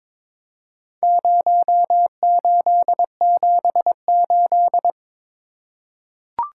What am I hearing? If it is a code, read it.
Morse code, heard as a keyed tone, about 22 wpm, 700 Hz. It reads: 0878
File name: practice / 22wpm / Four Letter-Number Combinations - 22wpm